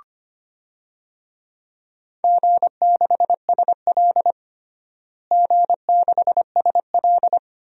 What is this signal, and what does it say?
Morse code, heard as a keyed tone, about 25 wpm, 700 Hz. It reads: G6SL G6SL
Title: DXCC-100 Call Signs - 4 Characters - Repeated 2x - 25wpm